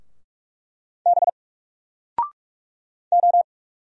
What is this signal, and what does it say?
Morse code, heard as a keyed tone, about 44 wpm, 700 Hz. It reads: B O
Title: Single Letter-Number 45wpm